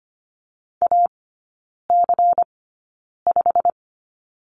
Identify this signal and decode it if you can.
Morse code, heard as a keyed tone, about 25 wpm, spaced as for 9 wpm, 700 Hz. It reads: AC5
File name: course / 25:10wpm / 093 Lesson 16e - Letter c - Three Characters 25wpm